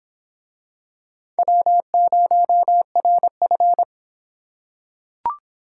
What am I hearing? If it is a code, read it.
Morse code, heard as a keyed tone, about 26 wpm, 700 Hz. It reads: W0RF